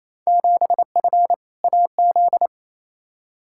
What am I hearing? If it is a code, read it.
Morse code, heard as a keyed tone, about 28 wpm, 700 Hz. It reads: 7F AZ
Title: Field Day Exchanges Encore - 28wpm